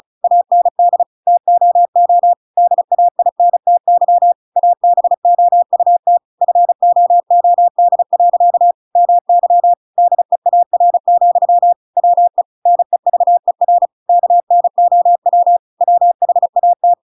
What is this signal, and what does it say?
Morse code, heard as a keyed tone, about 34 wpm, 700 Hz. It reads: AND TOO DAINTY ABOUT FOOD. MY DEAR, WE NEVER KNOW WHAT
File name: Pinocchio / Pinocchio ch7 35 35 ik1hge